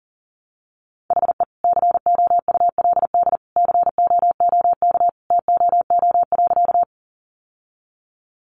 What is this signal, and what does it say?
Morse code, heard as a keyed tone, about 40 wpm, 700 Hz. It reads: HE COULD COOK TOO.
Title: Sentences from Top 700 Words 40wpm